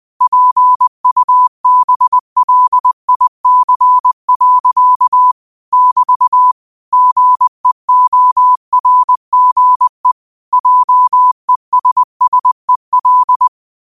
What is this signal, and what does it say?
Morse code, heard as a keyed tone, about 20 wpm, 1000 Hz. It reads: PUBLIC. <BT> GEORGE JESSEL
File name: 20wpm